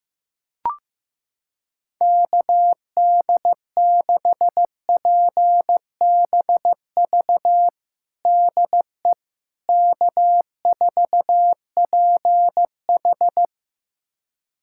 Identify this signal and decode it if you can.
Morse code, heard as a keyed tone, about 15 wpm, 700 Hz. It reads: KD6PBV DE K4PH
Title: USA Callsign DE Callsign 15wpm